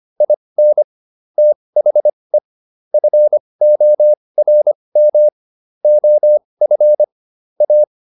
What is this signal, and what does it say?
Morse code, heard as a keyed tone, about 25 wpm, spaced as for 20 wpm, 600 Hz. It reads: IN THE FORM OF A